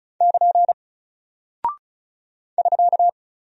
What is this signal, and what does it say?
Morse code, heard as a keyed tone, about 36 wpm, 700 Hz. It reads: <KN> <SK>
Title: Single Character (Letter:Number:Punctuation:Prosign) Alt 35wpm